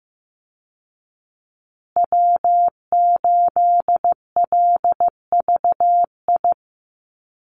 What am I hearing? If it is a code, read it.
Morse code, heard as a keyed tone, about 15 wpm, 700 Hz. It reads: W8LVI
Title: Call Signs Repeated 2x - 15wpm